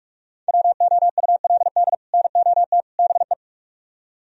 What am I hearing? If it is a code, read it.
Morse code, heard as a keyed tone, about 56 wpm, 700 Hz. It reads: WOULD NOT BE